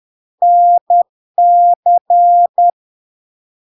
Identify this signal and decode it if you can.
Morse code, heard as a keyed tone, about 10 wpm, 700 Hz. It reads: NC
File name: LARK-FD-2022-Log-in-CW-10wpm-700-Hz